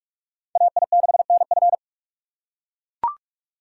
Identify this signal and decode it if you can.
Morse code, heard as a keyed tone, about 40 wpm, 700 Hz. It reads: AI6NF